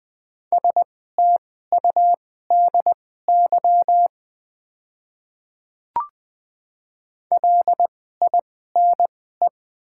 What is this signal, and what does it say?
Morse code, heard as a keyed tone, about 20 wpm, spaced as for 14 wpm, 700 Hz. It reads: STUDY LINE
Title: 2x Character Spacing - Top 300 Words - 20wpm